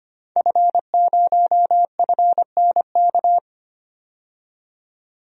Morse code, heard as a keyed tone, about 25 wpm, 690 Hz. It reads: F0FNK
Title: DXCC-100 Call Signs - 5 Characters - 25wpm